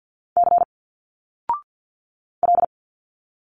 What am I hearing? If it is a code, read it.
Morse code, heard as a keyed tone, about 51 wpm, 700 Hz. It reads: C L
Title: Single Letters 50wpm